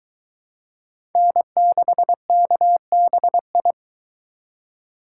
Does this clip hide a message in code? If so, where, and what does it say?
Yes: Morse code from 1.2 to 3.7 s, heard as a keyed tone, about 21 wpm, 700 Hz. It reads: N6KBI